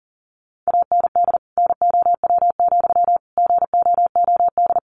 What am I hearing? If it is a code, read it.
Morse code, heard as a keyed tone, about 40 wpm, 700 Hz. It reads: AND NOW, GOOD
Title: Sentences from Top 300 Words 40wpm